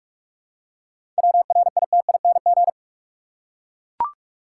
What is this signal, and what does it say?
Morse code, heard as a keyed tone, about 43 wpm, 700 Hz. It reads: WRITING
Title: Seven Letter Words - 45wpm